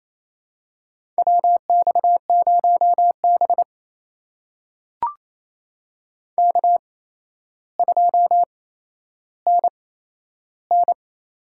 Morse code, heard as a keyed tone, about 28 wpm, spaced as for 12 wpm, 700 Hz. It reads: WX0B K2NN
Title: USA Call Signs - 4 Characters - 8x Character Spacing - 28wpm